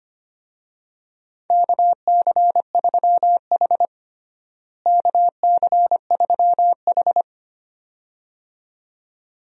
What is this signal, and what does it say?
Morse code, heard as a keyed tone, about 25 wpm, 700 Hz. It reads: KC3H KC3H
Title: USA Call Signs - 4 Characters - Repeated 2x - 25wpm